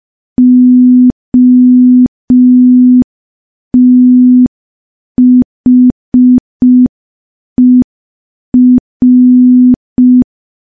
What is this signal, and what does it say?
Morse code, heard as a keyed tone, about 5 wpm, 250 Hz. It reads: OTHER
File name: morse / morse-training-05WPM